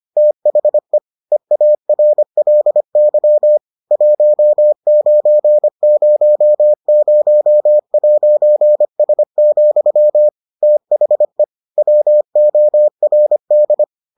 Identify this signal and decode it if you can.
Morse code, heard as a keyed tone, about 25 wpm, 600 Hz. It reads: THE EARLY 1900'S, THE WORD